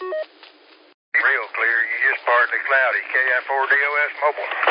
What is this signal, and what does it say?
Emergency Alert System (SAME) tones, heard as harsh buzzing bursts, with speech. It is heard from 1.1 to 4.1 s.